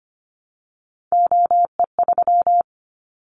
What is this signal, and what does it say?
Morse code, heard as a keyed tone, about 25 wpm, 700 Hz. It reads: OE3
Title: Callsign Prefixes for Top 25 DXCC - 25wpm